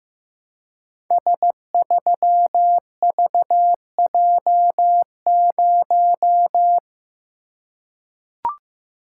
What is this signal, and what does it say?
Morse code, heard as a keyed tone, about 15 wpm, 700 Hz. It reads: S3VJ0
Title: Code Groups - 5 Letter:Numbers - 15wpm